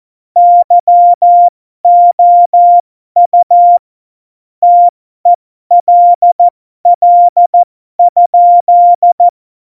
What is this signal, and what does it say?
Morse code, heard as a keyed tone, about 14 wpm, spaced as for 12 wpm, 700 Hz. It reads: YOU TELL?
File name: Morsekurs / Witze